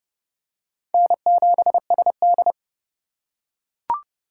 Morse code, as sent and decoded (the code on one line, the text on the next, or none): -. --... ... -..
N7SD